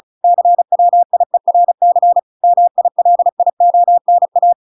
Morse code, heard as a keyed tone, about 34 wpm, 700 Hz. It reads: CWIERC MILIONA